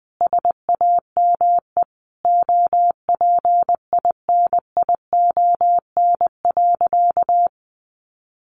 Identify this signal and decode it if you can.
Morse code, heard as a keyed tone, about 20 wpm, 700 Hz. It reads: SAME OPINION.
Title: Sentences from Top 1200 Words 20wpm